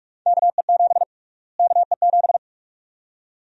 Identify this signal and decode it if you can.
Morse code, heard as a keyed tone, about 44 wpm, 700 Hz. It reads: KE7 KE7
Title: USA Call Sign Prefixes - Repeated 2x - 45wpm